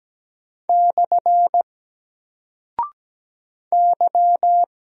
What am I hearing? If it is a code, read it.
Morse code, heard as a keyed tone, about 17 wpm, 700 Hz. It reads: / Y
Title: Single Character (Letter:Number:Punctuation:Prosign) Alt 17wpm